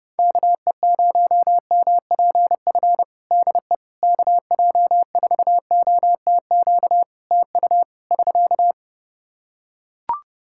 Morse code, heard as a keyed tone, about 30 wpm, 700 Hz. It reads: KE0MPF DE KJ4OTQ TU <SK>
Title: Practice QSO - Exchange 4 - Ending - 30wpm